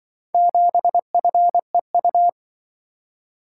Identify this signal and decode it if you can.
Morse code, heard as a keyed tone, about 25 wpm, 700 Hz. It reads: 7FEU
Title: Speed-Racing - Four Letter-Number Combinations - 20wpm